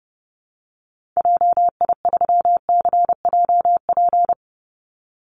Morse code, heard as a keyed tone, about 30 wpm, 700 Hz. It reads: JI3CJP
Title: Call Signs Repeated 2x - 30wpm